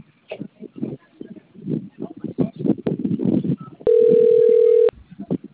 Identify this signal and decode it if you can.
Ringback tone (North America), 440 + 480 Hz.